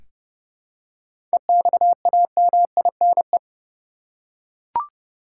Morse code, heard as a keyed tone, about 30 wpm, 700 Hz. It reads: EXAMINE
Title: Seven Letter Words - 30wpm